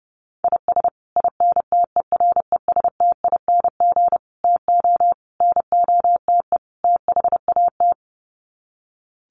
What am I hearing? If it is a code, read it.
Morse code, heard as a keyed tone, about 30 wpm, 700 Hz. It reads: IS INTERESTING TO NOTE THAT
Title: Sets of 5 Words Encore2 30wpm